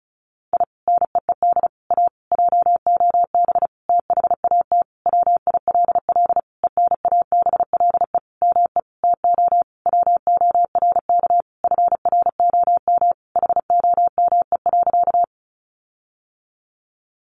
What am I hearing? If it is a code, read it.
Morse code, heard as a keyed tone, about 34 wpm, 700 Hz. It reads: I NEED A JOB THAT WILL ENABLE ME TO WORK FROM HOME.
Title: Sentences from Top 1500 Words 35wpm